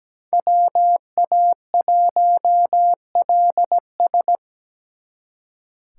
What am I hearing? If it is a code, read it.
Morse code, heard as a keyed tone, about 20 wpm, 700 Hz. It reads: WA1LS
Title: Speed Racing - USA 2x2 Call Signs 17wpm